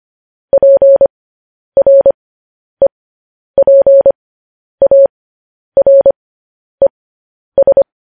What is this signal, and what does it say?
Morse code, heard as a keyed tone, about 25 wpm, spaced as for 10 wpm, 550 Hz. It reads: PREPARES
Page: news-headlines-in-morse-code-at-10-wpm